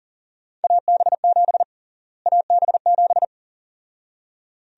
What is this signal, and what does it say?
Morse code, heard as a keyed tone, about 38 wpm, 700 Hz. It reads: AB7 AB7
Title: USA Call Sign Prefixes - Repeated 2x - 40wpm